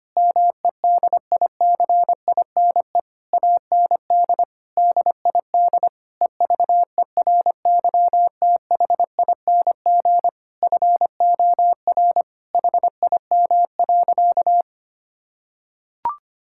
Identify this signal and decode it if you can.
Morse code, heard as a keyed tone, about 25 wpm, 700 Hz. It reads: MEDICINE AND DID EVERYTHING FOR HIM.
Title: Sentences from Top 1000 Words - Review Part 6 - 25wpm